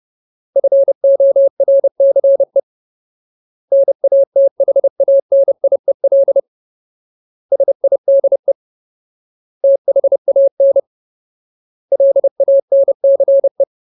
Morse code, heard as a keyed tone, about 30 wpm, 550 Hz. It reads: FORCE NATHANIEL SIDE THAN LANCE
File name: Ultimate Word List / 30WPM Ultimate